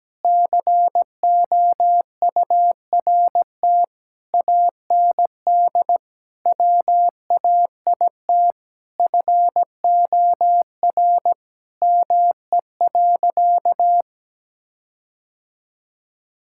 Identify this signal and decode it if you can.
Morse code, heard as a keyed tone, about 17 wpm, 700 Hz. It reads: COURT AND WAIT FOR ME.
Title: Sentences from Top 1600 Words 17wpm